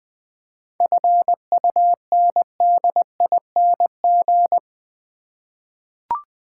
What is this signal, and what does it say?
Morse code, heard as a keyed tone, about 21 wpm, 700 Hz. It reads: FUNDING
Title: 7 Letter Words - First 3 Letters Given - 20wpm